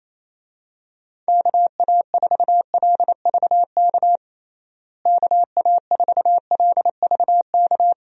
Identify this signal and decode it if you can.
Morse code, heard as a keyed tone, about 28 wpm, 700 Hz. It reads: KA4LVK KA4LVK
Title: USA Call Signs - 6 Characters - Repeated 3x - 28wpm